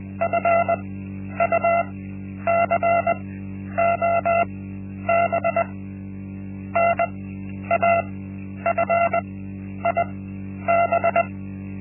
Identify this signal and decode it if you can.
Morse code, heard as a keyed tone, about 21 wpm, spaced as for 10 wpm, 680 Hz. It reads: FUCOB NAFIB
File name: Linksmakalnis-1